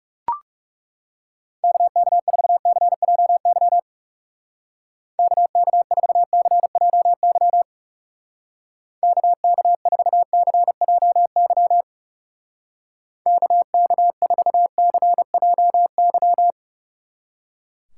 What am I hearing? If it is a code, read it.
Morse code, heard as a keyed tone, about 39 wpm, 700 Hz. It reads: KK4CJY KK4CJY KK4CJY KK4CJY